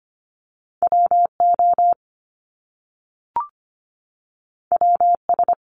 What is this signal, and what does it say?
Morse code, heard as a keyed tone, about 25 wpm, 700 Hz. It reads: WO WS
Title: Top Two Letter Combinations in Words 25wpm